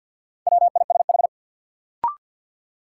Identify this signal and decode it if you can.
Morse code, heard as a keyed tone, about 48 wpm, 700 Hz. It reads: WISH
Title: Sentences from Top 1600 Words - New Word Drill Alt - 50wpm